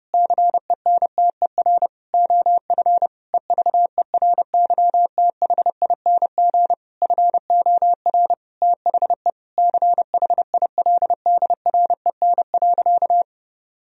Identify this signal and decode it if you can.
Morse code, heard as a keyed tone, about 30 wpm, 700 Hz. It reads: CENTER OF EVERYTHING FOR THE CHILDREN.
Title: Sentences from Top 1000 Words - Review Part 2 - 30wpm